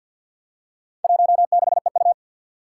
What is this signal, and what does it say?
Morse code, heard as a keyed tone, about 51 wpm, 700 Hz. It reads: 16EU
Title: Four Challenging Letter-Number Combinations 50wpm